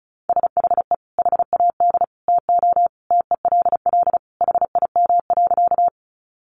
Morse code, heard as a keyed tone, about 34 wpm, 700 Hz. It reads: SHE HAD TO TELL HIM.
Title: Sentences from Top 200 Words 35wpm